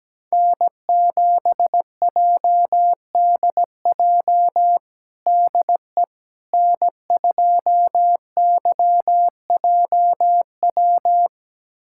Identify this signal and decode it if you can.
Morse code, heard as a keyed tone, about 17 wpm, 700 Hz. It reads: N7JDJ DE N2YJW
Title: USA Callsign DE Callsign 17wpm